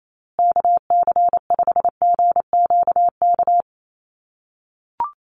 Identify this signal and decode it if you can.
Morse code, heard as a keyed tone, about 28 wpm, 700 Hz. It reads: KC5GQK
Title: USA Call Signs Repeated 2x 28wpm